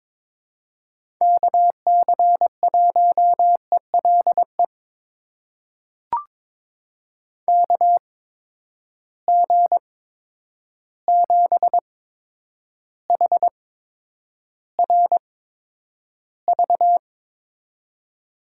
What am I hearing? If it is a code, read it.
Morse code, heard as a keyed tone, about 22 wpm, spaced as for 9 wpm, 700 Hz. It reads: KC1ELE KG7HRV
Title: USA Call Signs - 6 Characters - 8x Character Spacing - 22wpm